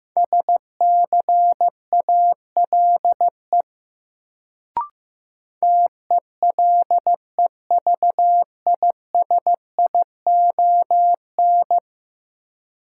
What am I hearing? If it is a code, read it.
Morse code, heard as a keyed tone, about 15 wpm, 700 Hz. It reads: SCALE TELEVISION
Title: Sentences from Top 1000 Words - Review Part 4 - New Word Drill - 15wpm